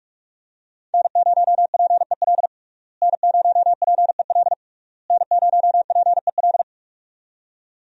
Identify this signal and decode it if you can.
Morse code, heard as a keyed tone, about 46 wpm, 700 Hz. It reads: N0PEL N0PEL N0PEL
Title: USA Call Signs - 5 Characters - Repeated 3x - 45wpm